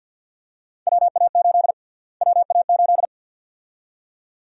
Morse code, heard as a keyed tone, about 46 wpm, 700 Hz. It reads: WA8 WA8
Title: USA Call Sign Prefixes - Repeated 2x - 50wpm